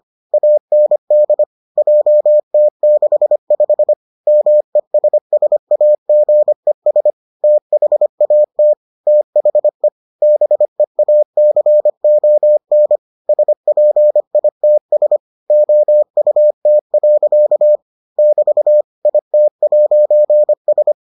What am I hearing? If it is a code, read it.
Morse code, heard as a keyed tone, about 25 wpm, 600 Hz. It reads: AND JT65 MESSAGES THAT THE BEACON SPITS OUT. <BT> IT'S